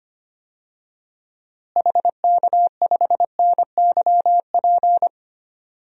Morse code, heard as a keyed tone, about 25 wpm, 700 Hz. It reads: HK5NYP
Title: DXCC-100 Call Signs - 6 Characters - Repeated 2x - 25wpm